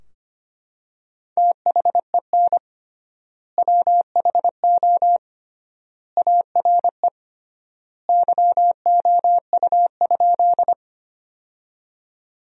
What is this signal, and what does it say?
Morse code, heard as a keyed tone, about 25 wpm, 700 Hz. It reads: THEN WHO ARE YOU?
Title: 3x Word-Spacing - Sentences from Top 100 Words - 25wpm